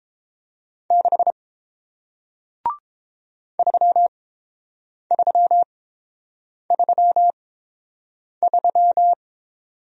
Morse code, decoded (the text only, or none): 6 3333